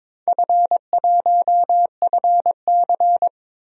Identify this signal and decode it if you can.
Morse code, heard as a keyed tone, about 21 wpm, 700 Hz. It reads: F1FC
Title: DXCC-100 Call Signs - 4 Characters - 22wpm